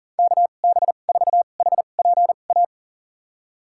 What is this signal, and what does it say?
Morse code, heard as a keyed tone, about 40 wpm, spaced as for 29 wpm, 700 Hz. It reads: KB4HPA
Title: USA Call Signs - 6 Characters - 2x Character Spacing - 40wpm